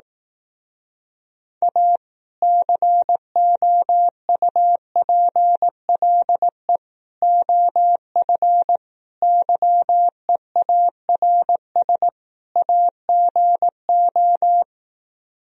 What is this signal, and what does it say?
Morse code, heard as a keyed tone, about 17 wpm, 700 Hz. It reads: A COUPLE OF YEARS AGO
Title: Speed-Racing - Sets of 5 Words 15wpm